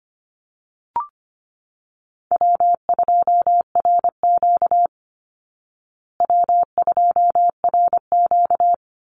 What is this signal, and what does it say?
Morse code, heard as a keyed tone, about 25 wpm, 700 Hz. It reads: W2RQ W2RQ
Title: Call Signs Repeated 3x - 25wpm